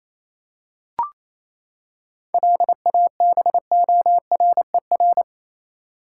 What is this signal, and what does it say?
Morse code, heard as a keyed tone, about 28 wpm, 700 Hz. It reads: LABORER
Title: Practice QSO - Exchange 3 - More Info - 28wpm